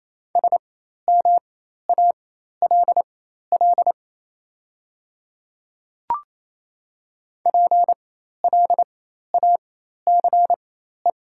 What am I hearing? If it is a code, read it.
Morse code, heard as a keyed tone, about 28 wpm, spaced as for 13 wpm, 700 Hz. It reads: SMALL PLACE
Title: 4x Character Spacing - Top 200 Words - 28wpm